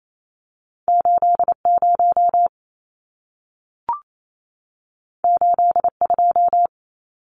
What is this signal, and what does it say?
Morse code, heard as a keyed tone, about 28 wpm, 700 Hz. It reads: 80 82
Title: Numbers - 2-Digit 28wpm